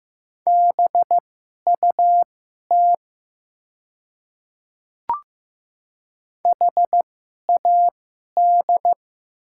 Morse code, heard as a keyed tone, about 15 wpm, spaced as for 11 wpm, 700 Hz. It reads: BUT HAD
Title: 2x Character Spacing - Top 100 Words - 15wpm